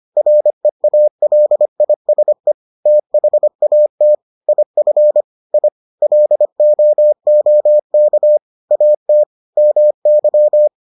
Morse code, heard as a keyed tone, about 25 wpm, 600 Hz. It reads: REALISE THAT IF I LOOK AT MY